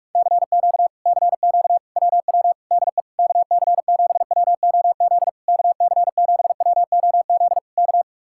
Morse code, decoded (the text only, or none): CQ CQ WW DE KC7WOZ KC7WOZ K